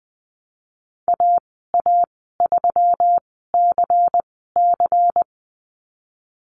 Morse code, heard as a keyed tone, about 19 wpm, spaced as for 14 wpm, 700 Hz. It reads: AA3CC
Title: 140 Lesson 23g - Letter f - Call Signs 20wpm